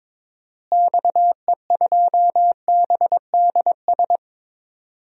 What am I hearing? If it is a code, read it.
Morse code, heard as a keyed tone, about 22 wpm, 700 Hz. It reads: XE2BDS
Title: DXCC-100 Call Signs - 6 Characters - Repeated 3x - 22wpm